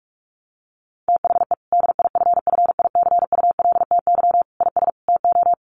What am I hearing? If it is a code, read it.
Morse code, heard as a keyed tone, about 45 wpm, 700 Hz. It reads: THE DIFFICULTY IS TO
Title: Sentences from Top 1500 Words 45wpm